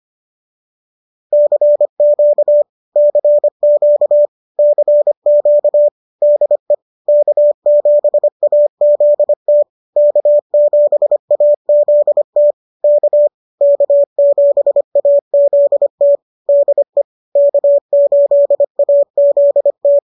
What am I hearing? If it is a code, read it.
Morse code, heard as a keyed tone, about 25 wpm, 590 Hz. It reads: CQ CQ CQ DE K7AZT K7AZT K K7AZT DE K8AZT